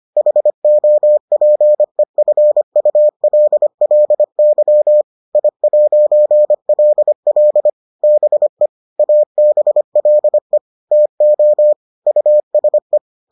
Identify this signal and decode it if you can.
Morse code, heard as a keyed tone, about 25 wpm, 600 Hz. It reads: HOPEFULLY I'LL BE ABLE TO USE